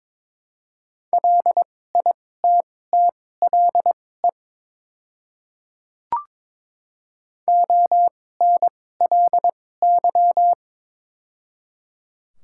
Morse code, heard as a keyed tone, about 22 wpm, spaced as for 16 wpm, 700 Hz. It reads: LITTLE ONLY
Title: 2x Character Spacing - Top 200 Words - 22wpm